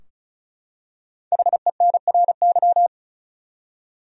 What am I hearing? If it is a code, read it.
Morse code, heard as a keyed tone, about 36 wpm, 700 Hz. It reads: HENRY
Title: USA Common Names - Max 5 Letters - 35wpm